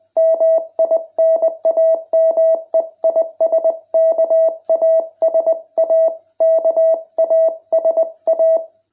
Morse code, heard as a keyed tone, about 20 wpm, 660 Hz. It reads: MI NAME IS KASA KASA